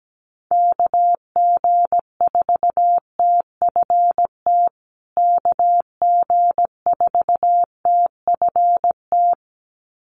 Morse code, heard as a keyed tone, about 17 wpm, 700 Hz. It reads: KG4TFT KG4TFT
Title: USA Call Signs Repeated 2x 17wpm